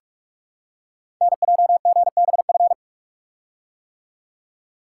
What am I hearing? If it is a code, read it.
Morse code, heard as a keyed tone, about 44 wpm, 690 Hz. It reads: NJGBF
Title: Code Groups - 5 Letters - 45wpm